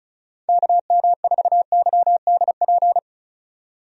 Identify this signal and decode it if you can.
Morse code, heard as a keyed tone, about 36 wpm, 700 Hz. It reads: KM4YDP
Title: USA Call Signs - 6 Characters - 35wpm